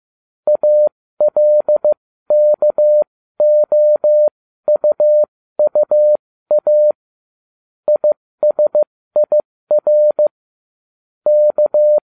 Morse code, heard as a keyed tone, about 15 wpm, spaced as for 13 wpm, 600 Hz. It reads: ALKOUUA ISIR K